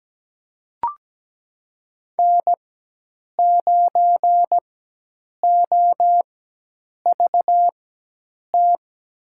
Morse code, heard as a keyed tone, about 17 wpm, spaced as for 8 wpm, 700 Hz. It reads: N9OVT
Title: USA Call Signs - 5 Characters - 4x Character Spacing - 17wpm